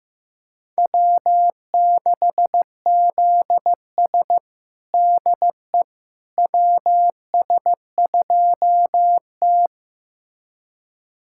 Morse code, heard as a keyed tone, about 15 wpm, 700 Hz. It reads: W6ZS DE WS2T
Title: USA Callsign DE Callsign - 4 Characters - 15wpm